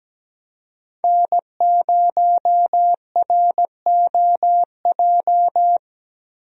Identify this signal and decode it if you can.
Morse code, heard as a keyed tone, about 17 wpm, 700 Hz. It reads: N0ROJ
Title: USA Call Signs - 5 Characters - Repeated 3x - 17wpm